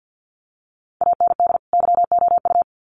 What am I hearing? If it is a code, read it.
Morse code, heard as a keyed tone, about 50 wpm, 700 Hz. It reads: AND YOU
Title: Sets of 2 Words 50wpm